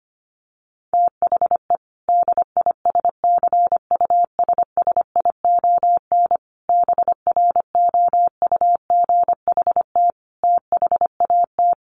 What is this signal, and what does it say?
Morse code, heard as a keyed tone, about 25 wpm, 700 Hz. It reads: THE DISCUSSION BROUGHT THAT
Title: Sentences from Top 1100 Words 25wpm